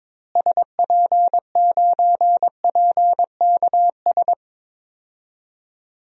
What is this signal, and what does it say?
Morse code, heard as a keyed tone, about 22 wpm, 700 Hz. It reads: SP9PKS
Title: DXCC-100 Call Signs - 6 Characters - Repeated 2x - 22wpm